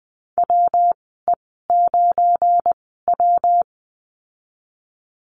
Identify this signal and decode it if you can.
Morse code, heard as a keyed tone, about 20 wpm, spaced as for 15 wpm, 700 Hz. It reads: WE9W
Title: 173 Lesson 28g - Number 9 - Call Signs 20wpm